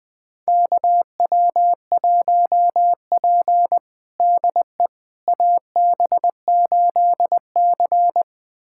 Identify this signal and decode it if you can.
Morse code, heard as a keyed tone, about 21 wpm, 700 Hz. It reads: KW1P DE AB8C
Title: USA Callsign DE Callsign - 4 Characters - 20wpm